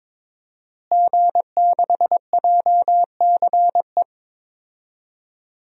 Morse code, heard as a keyed tone, about 22 wpm, 700 Hz. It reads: G6JCE
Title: DXCC-100 Call Signs - 5 Characters - Repeated 2x - 22wpm